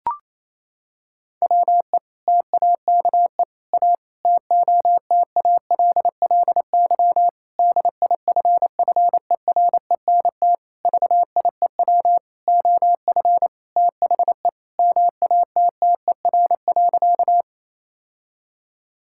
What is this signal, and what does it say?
Morse code, heard as a keyed tone, about 28 wpm, 700 Hz. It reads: WE TAKE A TOTALLY DIFFERENT VIEW OF THE MATTER.
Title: Sentences from Top 1600 Words 28wpm